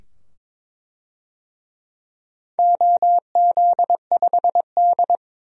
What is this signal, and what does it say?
Morse code, heard as a keyed tone, about 22 wpm, 700 Hz. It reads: OZ5D
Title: DXCC-100 Call Signs - 4 Characters - 22wpm